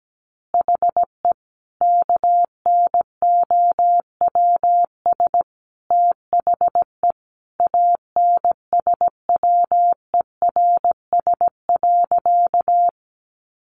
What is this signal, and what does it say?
Morse code, heard as a keyed tone, about 17 wpm, 700 Hz. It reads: HE KNOWS THE ANSWERS.
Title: Additional Easy Phrases 17wpm